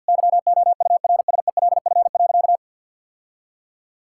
Morse code, heard as a keyed tone, about 50 wpm, 700 Hz. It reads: YOURSELF.
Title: Sentences from Top 1600 Words 50wpm